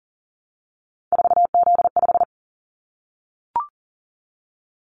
Morse code, heard as a keyed tone, about 39 wpm, 700 Hz. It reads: 4Z5
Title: Callsign Prefixes for Top 50 DXCC - 40wpm